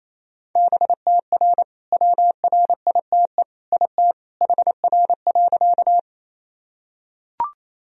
Morse code, heard as a keyed tone, about 27 wpm, 700 Hz. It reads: BTR WRITE IT HR.
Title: Sentences from Top 400 Words and Abbreviations 28wpm